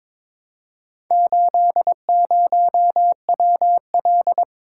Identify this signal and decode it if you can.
Morse code, heard as a keyed tone, about 22 wpm, 700 Hz. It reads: 80WL
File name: practice / 22wpm / Four Letter-Number Combinations - 22wpm